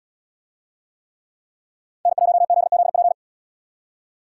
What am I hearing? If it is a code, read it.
Morse code, heard as a keyed tone, about 78 wpm, 700 Hz. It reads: N1ZZP